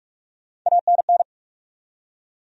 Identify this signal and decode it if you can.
Morse code, heard as a keyed tone, about 46 wpm, 700 Hz. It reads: ANN
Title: USA Common Names - Max 4 Letters - 45wpm